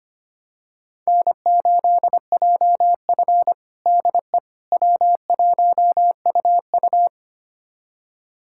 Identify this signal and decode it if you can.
Morse code, heard as a keyed tone, about 25 wpm, 700 Hz. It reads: N8JF DE W1UU
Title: USA Callsign DE Callsign - 4 Characters - 25wpm